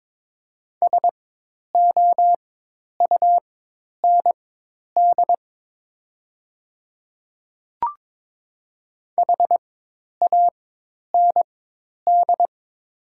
Morse code, heard as a keyed tone, about 22 wpm, spaced as for 10 wpm, 700 Hz. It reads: SOUND HAND
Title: 4x Character Spacing - Top 300 Words - 22wpm